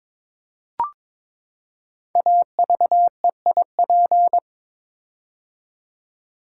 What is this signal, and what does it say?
Morse code, heard as a keyed tone, about 22 wpm, 700 Hz. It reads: AVEIP
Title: Code Groups - 5 Letters - 22wpm